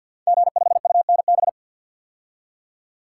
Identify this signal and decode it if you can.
Morse code, heard as a keyed tone, about 50 wpm, 700 Hz. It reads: G5UNB
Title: Code Groups - 5 Letter:Numbers - 50wpm